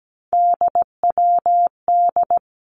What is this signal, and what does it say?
Morse code, heard as a keyed tone, about 17 wpm, 700 Hz. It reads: DWD
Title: Three Challenging Letter-Number Combinations 17wpm